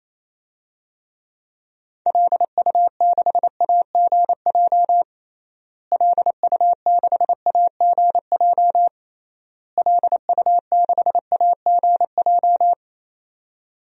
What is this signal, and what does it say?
Morse code, heard as a keyed tone, about 28 wpm, 700 Hz. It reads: LU6AGJ LU6AGJ LU6AGJ
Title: DXCC-100 Call Signs - 6 Characters - Repeated 3x - 28wpm